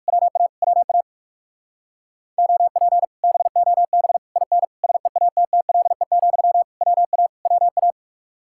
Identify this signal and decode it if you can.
Morse code, heard as a keyed tone, about 56 wpm, 700 Hz. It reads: WA WA OP BOB IN SEATTLE, WA WA
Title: Speed-Racing Common QSO Phrases - 45wpm